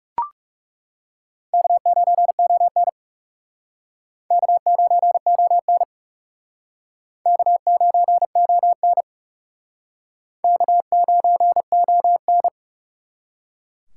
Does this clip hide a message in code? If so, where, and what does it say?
Yes: Morse code from 1.5 to 12.5 s, heard as a keyed tone, about 40 wpm, 700 Hz. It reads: K9ON K9ON K9ON K9ON